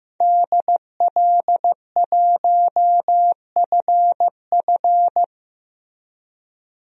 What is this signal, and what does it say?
Morse code, heard as a keyed tone, about 15 wpm, 700 Hz. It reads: DL1FF
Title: DXCC-100 Call Signs - 5 Characters - Repeated 2x - 15wpm